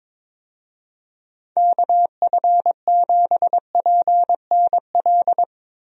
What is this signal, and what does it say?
Morse code, heard as a keyed tone, about 22 wpm, 700 Hz. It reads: KF7PNL
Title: USA Call Signs - 6 Characters - Repeated 2x - 22wpm